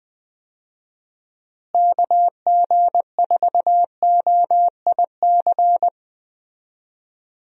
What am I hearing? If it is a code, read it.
Morse code, heard as a keyed tone, about 20 wpm, 700 Hz. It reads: KG4OIC
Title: USA Call Signs - 6 Characters - Repeated 2x - 20wpm